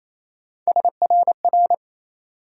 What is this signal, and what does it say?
Morse code, heard as a keyed tone, about 27 wpm, 700 Hz. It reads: SRR
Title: USA Call Sign Postfixes - 3 Letters 28wpm